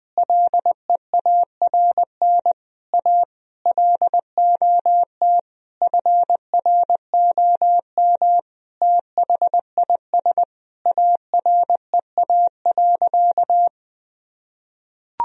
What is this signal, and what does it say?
Morse code, heard as a keyed tone, about 20 wpm, 700 Hz. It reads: LEARN A LOT FROM THIS AREA.
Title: Sentences from Top 1000 Words - Review Part 2 - 20wpm